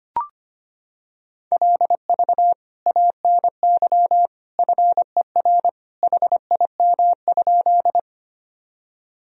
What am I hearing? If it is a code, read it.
Morse code, heard as a keyed tone, about 25 wpm, 700 Hz. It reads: LV ANY FER HIM?
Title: Sentences from Top 400 Words and Abbreviations 25wpm